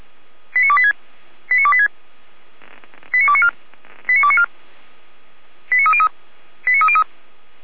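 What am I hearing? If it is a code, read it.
Five-tone selective call (ZVEI), heard as quick runs of five tones, tones of about 70 ms. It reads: 89286 89286 89294 89294 89303 89303
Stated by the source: ZVEI five-tone selective call 89270 received by SDR.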